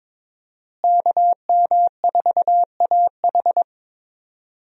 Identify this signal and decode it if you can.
Morse code, heard as a keyed tone, about 22 wpm, 700 Hz. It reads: KM4AH
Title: USA Call Signs - 5 Characters - Repeated 2x - 22wpm